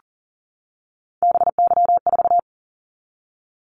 Morse code, heard as a keyed tone, about 42 wpm, 700 Hz. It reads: BY4
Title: Callsign Prefixes for Top 100 DXCC - 40wpm